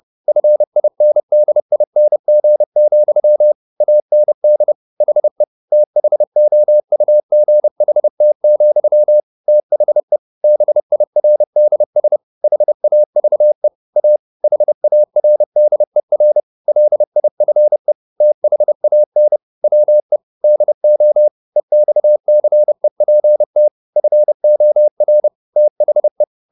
Morse code, heard as a keyed tone, about 30 wpm, 600 Hz. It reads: FINDING, AND HE THOUGHT, THE BIRDS HAVE A HARDER LIFE THAN WE DO EXCEPT FOR THE
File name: the old man and the sea / the old man and the sea 1 30wpm